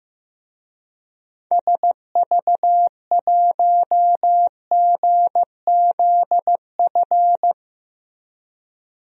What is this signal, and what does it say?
Morse code, heard as a keyed tone, about 15 wpm, 700 Hz. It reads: SV1GZF
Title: DXCC-100 Call Signs - 6 Characters - Repeated 3x - 15wpm